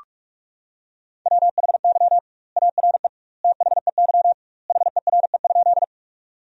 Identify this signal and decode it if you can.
Morse code, heard as a keyed tone, about 46 wpm, 700 Hz. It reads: WHY ARE THEY HERE?
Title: 2x Word-Spacing - Sentences from Top 200 Words - 45wpm